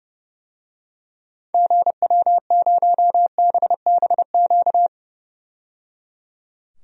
Morse code, heard as a keyed tone, about 30 wpm, 700 Hz. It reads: GW0BBQ
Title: DXCC-100 Call Signs - 6 Characters - 30wpm